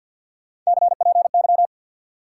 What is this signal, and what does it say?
Morse code, heard as a keyed tone, about 49 wpm, 700 Hz. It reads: CPY
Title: Top 180 QSO Elements 50wpm